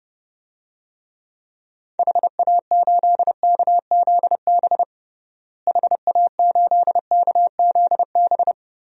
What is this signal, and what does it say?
Morse code, heard as a keyed tone, about 30 wpm, 700 Hz. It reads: HA8KZB HA8KZB
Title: DXCC-100 Call Signs - 6 Characters - Repeated 3x - 30wpm